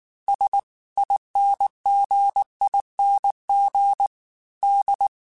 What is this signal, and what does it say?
Morse code, heard as a keyed tone, about 19 wpm, 800 Hz. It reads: SINGING D
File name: Special INGs-mp3s / Special INGs-mp3s / Dble Ings sp @18